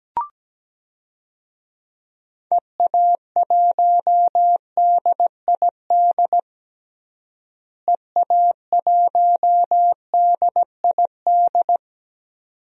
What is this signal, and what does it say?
Morse code, heard as a keyed tone, about 17 wpm, 700 Hz. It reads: EA1DID EA1DID
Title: DXCC-100 Call Signs - 6 Characters - Repeated 2x - 17wpm